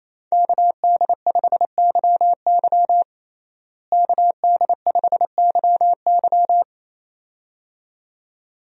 Morse code, heard as a keyed tone, about 28 wpm, 700 Hz. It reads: KD5YY KD5YY
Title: USA Call Signs - 5 Characters - Repeated 2x - 28wpm